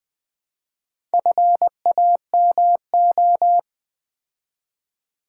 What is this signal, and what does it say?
Morse code, heard as a keyed tone, about 20 wpm, 700 Hz. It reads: FAMO
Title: Build A Word - 2 Letters At A Time - 6-Letter Words - 20wpm